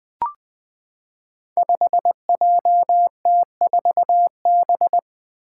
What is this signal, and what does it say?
Morse code, heard as a keyed tone, about 20 wpm, 700 Hz. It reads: 5JT4B
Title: Five Challenging Letter-Number Combinations 20wpm